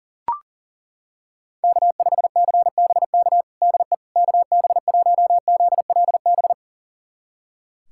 Morse code, heard as a keyed tone, about 40 wpm, 700 Hz. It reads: K5CBK DE KB1ZLB
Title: USA Callsign DE Callsign 40wpm